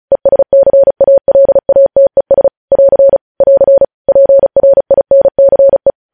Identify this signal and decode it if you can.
Morse code, heard as a keyed tone, about 36 wpm, 550 Hz. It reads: ESCALATES <AR> <AR> PRINCE
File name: podcast / News headlines-1x-35x35